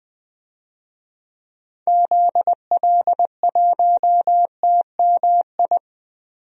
Morse code, heard as a keyed tone, about 21 wpm, 700 Hz. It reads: ZL1TMI
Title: DXCC-100 Call Signs - 6 Characters - 20wpm